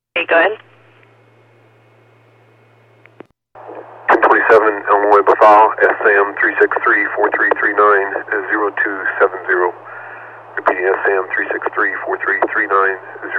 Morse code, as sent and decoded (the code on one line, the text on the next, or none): none